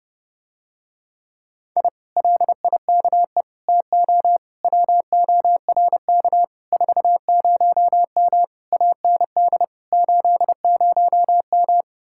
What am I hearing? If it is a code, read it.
Morse code, heard as a keyed tone, about 30 wpm, 700 Hz. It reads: I LIKE TO WORK 40M AND 80M